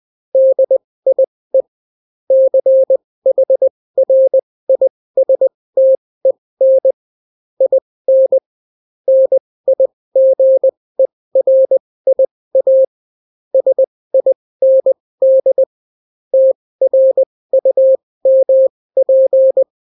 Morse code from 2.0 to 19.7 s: CHRISTEN IN NIGERIA SIND TRUMP